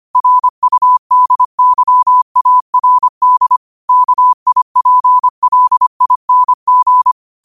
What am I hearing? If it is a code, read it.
Morse code, heard as a keyed tone, about 25 wpm, 1000 Hz. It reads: RUDYARD KIPLING
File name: qotdfiles / 25wpm